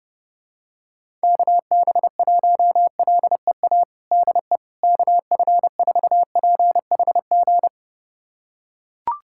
Morse code, heard as a keyed tone, about 30 wpm, 700 Hz. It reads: KB1LEA DE KF4PHG
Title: USA Callsign DE Callsign 30wpm